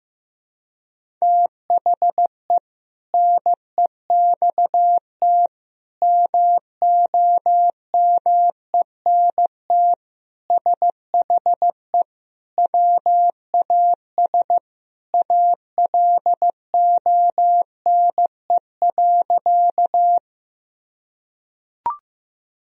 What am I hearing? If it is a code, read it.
Morse code, heard as a keyed tone, about 15 wpm, 700 Hz. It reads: THE NEXT MOMENT SHE WAS ALONE.
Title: Sentences from Top 1000 Words - Review Part 1 - 15wpm